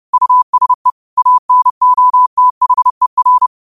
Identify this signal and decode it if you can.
Morse code, heard as a keyed tone, about 30 wpm, 1000 Hz. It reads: USE ANOTHER